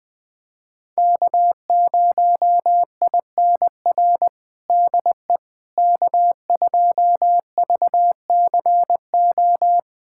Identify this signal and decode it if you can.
Morse code, heard as a keyed tone, about 20 wpm, 700 Hz. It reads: K0INR DE K2VCO
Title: USA Callsign DE Callsign - 5 Characters - 20wpm